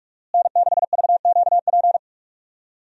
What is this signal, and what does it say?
Morse code, heard as a keyed tone, about 45 wpm, 700 Hz. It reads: N6VQP
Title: USA Call Signs - 5 Characters - Repeated 3x - 45wpm